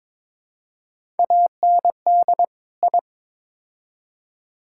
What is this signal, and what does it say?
Morse code, heard as a keyed tone, about 22 wpm, 700 Hz. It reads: AND I
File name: practice / 22wpm / Build A Phrase - 5-Word Phrases - 22wpm